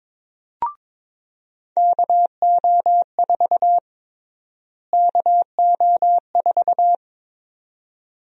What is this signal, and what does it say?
Morse code, heard as a keyed tone, about 22 wpm, 700 Hz. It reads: KO4 KO4
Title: USA Call Sign Prefixes - Repeated 2x - 22wpm